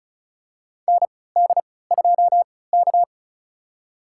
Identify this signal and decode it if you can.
Morse code, heard as a keyed tone, about 34 wpm, spaced as for 20 wpm, 700 Hz. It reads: ND2K
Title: USA Call Signs - 4 Characters - 3x Character Spacing - 35wpm